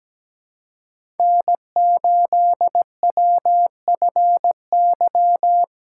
Morse code, heard as a keyed tone, about 17 wpm, 700 Hz. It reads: N8WFY
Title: USA Call Signs - 5 Characters - 3x Character Spacing - 17wpm